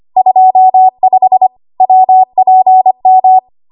Morse code, heard as a keyed tone, about 25 wpm, 750 Hz. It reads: 25 WPM
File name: Morse / 25WPM